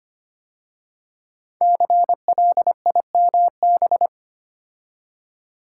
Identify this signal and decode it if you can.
Morse code, heard as a keyed tone, about 25 wpm, 700 Hz. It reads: CLIMB
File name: 5 Letter Words - First 3 Letters Given - 25wpm